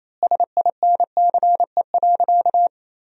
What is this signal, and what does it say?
Morse code, heard as a keyed tone, about 28 wpm, 700 Hz. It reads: SINCE.
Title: 3x Word-Spacing - Sentences from Top 300 Words - 28wpm